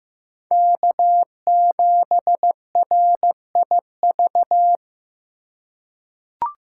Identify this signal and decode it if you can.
Morse code, heard as a keyed tone, about 15 wpm, 700 Hz. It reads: K7RIV
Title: USA Call Signs - 4x Character Spacing - 15wpm